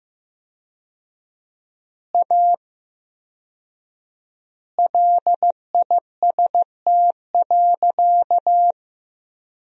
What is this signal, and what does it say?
Morse code, heard as a keyed tone, about 15 wpm, 700 Hz. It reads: A LIST.
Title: 4x Word-Spacing - Sentences from Top 400 Words - 15wpm